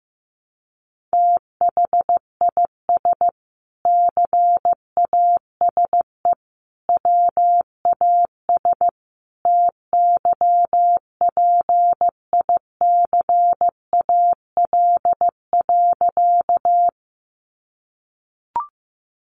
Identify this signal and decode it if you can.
Morse code, heard as a keyed tone, about 15 wpm, 700 Hz. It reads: THIS CASE WAS TYPICAL.
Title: Sentences from Top 1100 Words 15wpm